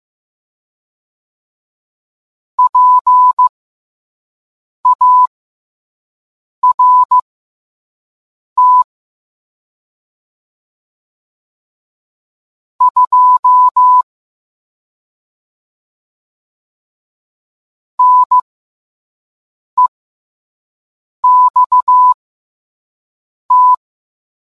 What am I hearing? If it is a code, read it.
Morse code, heard as a keyed tone, about 15 wpm, spaced as for 5 wpm, 1000 Hz. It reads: PART 2 NEXT